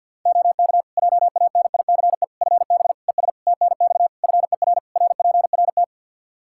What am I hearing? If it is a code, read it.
Morse code, heard as a keyed tone, about 53 wpm, 700 Hz. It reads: OK JANICE FB ES TNX FER RPRT
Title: K1USN SST QSO Practice 50wpm